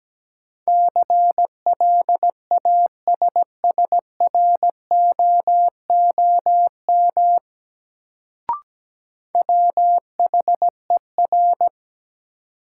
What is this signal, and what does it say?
Morse code, heard as a keyed tone, about 17 wpm, 700 Hz. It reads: CLASSROOM WHER
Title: Sentences from Top 1500 Words - New Word Drill - 17wpm